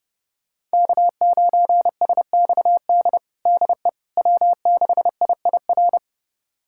Morse code, heard as a keyed tone, about 32 wpm, 700 Hz. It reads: K9SXD DE W6IIR
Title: USA Callsign DE Callsign - 5 Characters - 30wpm